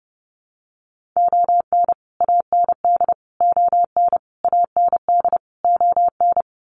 Morse code, heard as a keyed tone, about 30 wpm, 700 Hz. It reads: ON AND ON AND ON
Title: Sets of 5 Words Encore2 30wpm